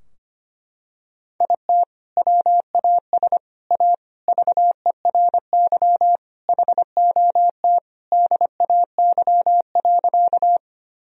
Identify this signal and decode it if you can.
Morse code, heard as a keyed tone, about 25 wpm, 700 Hz. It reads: IT WAS A VERY HOT DAY.
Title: Sentences from Top 1000 Words - Review Part 3 - 25wpm